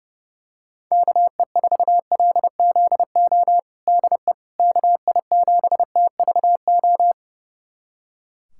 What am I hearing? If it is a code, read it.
Morse code, heard as a keyed tone, about 32 wpm, 700 Hz. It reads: KE4LZO DE KI7TVO
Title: USA Callsign DE Callsign - 6 Characters - 30wpm